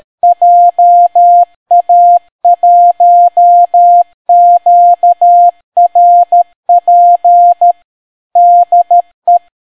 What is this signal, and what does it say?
Morse code, heard as a keyed tone, about 13 wpm, 700 Hz. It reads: JA1QRP DE